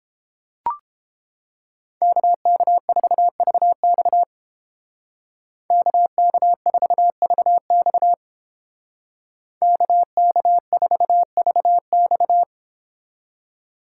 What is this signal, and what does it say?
Morse code, heard as a keyed tone, about 30 wpm, 700 Hz. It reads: KK4VX KK4VX KK4VX